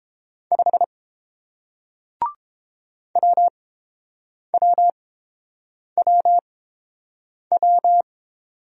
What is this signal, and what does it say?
Morse code, heard as a keyed tone, about 30 wpm, spaced as for 7 wpm, 700 Hz. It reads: 5 WWWW